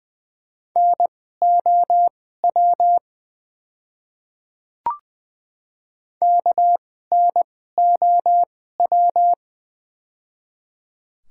Morse code, heard as a keyed tone, about 20 wpm, spaced as for 15 wpm, 700 Hz. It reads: NOW KNOW
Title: 2x Character Spacing - Top 100 Words - 20wpm